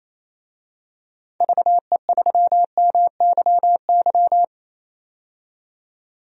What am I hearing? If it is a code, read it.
Morse code, heard as a keyed tone, about 28 wpm, 700 Hz. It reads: VE3MYY
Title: DXCC-100 Call Signs - 6 Characters - 28wpm